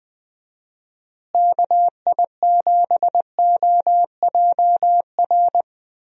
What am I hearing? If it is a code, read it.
Morse code, heard as a keyed tone, about 20 wpm, 700 Hz. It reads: KI7OJR
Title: USA Call Signs - 6 Characters - Repeated 2x - 20wpm